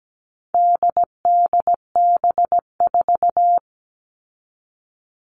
Morse code, heard as a keyed tone, about 17 wpm, 700 Hz. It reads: DDB4
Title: All Cut Numbers - 4-Digit 17wpm